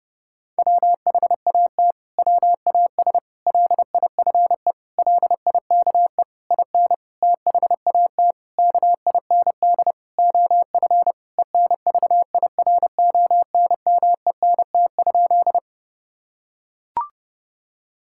Morse code, heard as a keyed tone, about 30 wpm, 700 Hz. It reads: WHAT WAS LIFE LIKE IN THAT KIND OF ENVIRONMENT?
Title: Sentences from Top 1000 Words - Review Part 5 - 30wpm